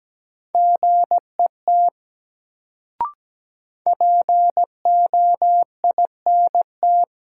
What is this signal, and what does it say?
Morse code, heard as a keyed tone, about 17 wpm, 700 Hz. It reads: GET POINT
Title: Sentences from Top 200 Words - New Word Drill - 17wpm